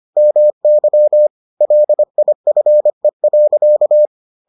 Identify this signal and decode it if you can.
Morse code, heard as a keyed tone, about 25 wpm, 600 Hz. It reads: MY LIFE.